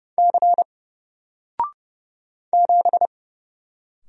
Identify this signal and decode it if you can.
Morse code, heard as a keyed tone, about 30 wpm, 700 Hz. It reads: C 7
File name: Single-Letter-Number-30wpm